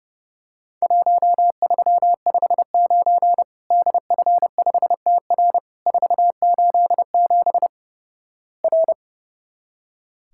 Morse code, heard as a keyed tone, about 30 wpm, 700 Hz. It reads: 1359 DF5TR 487 R
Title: Worked All Europe DX Contest - QTC Exchange - 30wpm